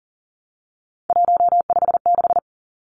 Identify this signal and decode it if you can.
Morse code, heard as a keyed tone, about 40 wpm, 700 Hz. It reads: 156
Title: Numbers - 3-Digit 40wpm